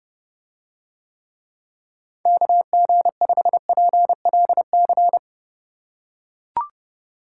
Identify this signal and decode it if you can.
Morse code, heard as a keyed tone, about 32 wpm, 700 Hz. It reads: KG5PLC